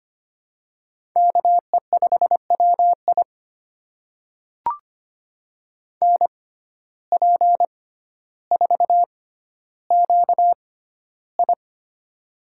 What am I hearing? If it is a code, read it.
Morse code, heard as a keyed tone, about 25 wpm, spaced as for 13 wpm, 700 Hz. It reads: KE5WI NP4QI
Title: USA Call Signs - 5 Characters - 6x Character Spacing - 25wpm